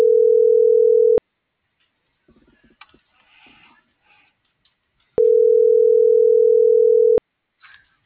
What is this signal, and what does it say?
Ringback tone (North America), 440 + 480 Hz.